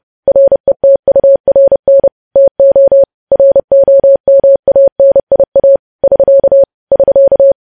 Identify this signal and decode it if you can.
Morse code, heard as a keyed tone, about 30 wpm, 550 Hz. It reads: RETURN TO ROMANIA <SK> <SK>
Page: news headlines in morse code at 30 wpm